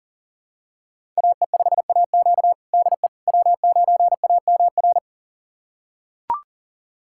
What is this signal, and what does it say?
Morse code, heard as a keyed tone, about 40 wpm, 700 Hz. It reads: AE5AQ DE W9AMR
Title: USA Callsign DE Callsign - 5 Characters - 40wpm